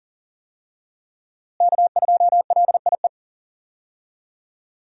Morse code, heard as a keyed tone, about 40 wpm, 700 Hz. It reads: K2LIE
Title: USA Call Signs - 5 Characters - Repeated 2x - 40wpm